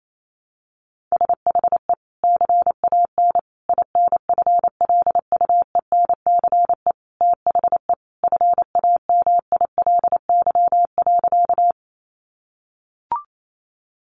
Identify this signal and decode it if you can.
Morse code, heard as a keyed tone, about 28 wpm, 700 Hz. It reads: SHE CAN INFLUENCE THE FAMILY.
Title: Sentences from Top 1100 Words 28wpm